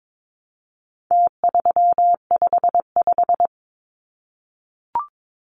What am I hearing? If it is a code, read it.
Morse code, heard as a keyed tone, about 22 wpm, 700 Hz. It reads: T355
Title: Numbers - Contest Style - 4-Digit 22wpm